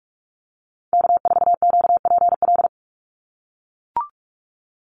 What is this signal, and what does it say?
Morse code, heard as a keyed tone, about 44 wpm, 700 Hz. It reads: K4QPL
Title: Call Signs - 45wpm